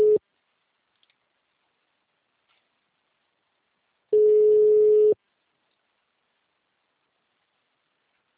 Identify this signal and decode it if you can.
Ringback tone (Europe, 425 Hz).